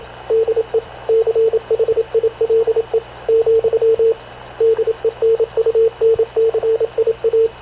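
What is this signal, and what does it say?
Morse code, heard as a keyed tone, about 27 wpm, 450 Hz, with radio noise. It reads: DE CHILE , DENUNCIA
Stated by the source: Havana Radio CLA press (PX) broadcast, recorded by Radio Officer Paul Dumesnil aboard mv CNAV QUEST/CZDO.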